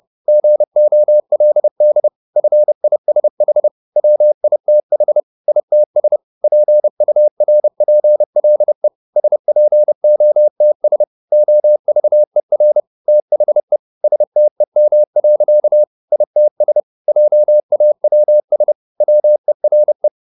Morse code, heard as a keyed tone, about 30 wpm, 600 Hz. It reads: GOLD FISH WITH ITS PURPLE SPOTS OVER THE STEM. ITS JAWS WERE